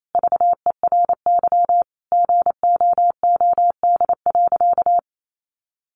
Morse code, heard as a keyed tone, about 28 wpm, 700 Hz. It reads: VERY GOOD.
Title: Sentences from Top 1400 Words 28wpm